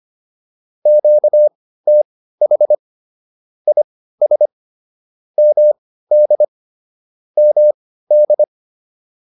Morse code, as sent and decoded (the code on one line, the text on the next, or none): --.- - .... / .. ... / -- -.. / -- -..
QTH IS MD MD